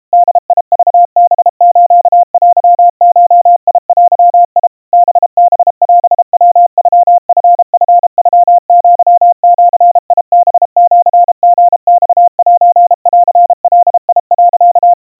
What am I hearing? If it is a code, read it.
Morse code (Japanese Wabun code), heard as a keyed tone, about 32 wpm, 700 Hz. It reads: ボクハステゴデ ハハオヤノチチノアジハシリマセンガ、
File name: kawa / Morse sample / kamyu 120 cpm 700 Hz Crispy